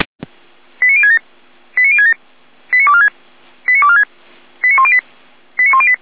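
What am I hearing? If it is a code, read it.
Five-tone selective call (ZVEI), heard as quick runs of five tones, tones of about 80 ms. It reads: 89067 89067 89246 89246 89108 89108
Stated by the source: ZVEI five-tone call 89169 received on an SDR.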